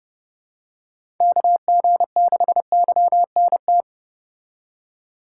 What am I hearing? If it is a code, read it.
Morse code, heard as a keyed tone, about 29 wpm, 700 Hz. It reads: KG6YNT